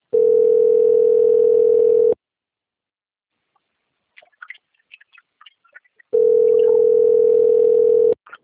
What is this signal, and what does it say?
Ringback tone (North America), 440 + 480 Hz.
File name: out- - - - - .10668